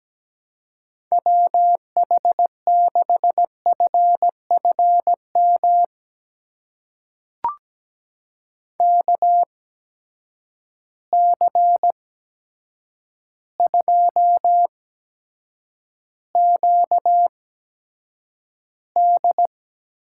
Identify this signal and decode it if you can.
Morse code, heard as a keyed tone, about 17 wpm, spaced as for 8 wpm, 700 Hz. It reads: WH6FFM KC2QD
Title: USA Call Signs - 6 Characters - 8x Character Spacing - 17wpm